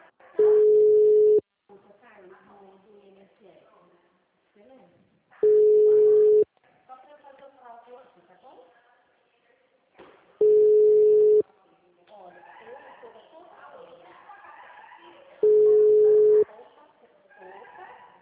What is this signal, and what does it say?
Ringback tone (Europe, 425 Hz), with speech.
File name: out- -2018- - - .39425